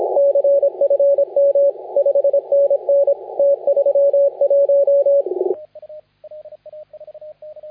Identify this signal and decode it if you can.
Morse code, heard as a keyed tone, about 24 wpm, 560 Hz, with radio noise. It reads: CFM 5NN T31
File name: la4c / sac-cw-2018 / CONTEST AUDIO 1600Z